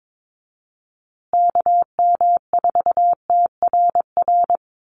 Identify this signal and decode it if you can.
Morse code, heard as a keyed tone, about 22 wpm, 700 Hz. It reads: KM4TRR
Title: USA Call Signs Repeated 2x 22wpm